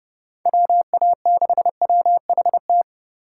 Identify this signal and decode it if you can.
Morse code, heard as a keyed tone, about 29 wpm, 700 Hz. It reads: WA6WHT
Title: USA Call Signs - 8x Character Spacing - 30wpm